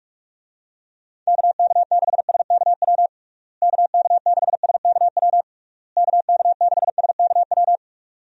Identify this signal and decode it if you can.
Morse code, heard as a keyed tone, about 46 wpm, 700 Hz. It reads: KK6SKW KK6SKW KK6SKW
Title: USA Call Signs - 6 Characters - Repeated 3x - 45wpm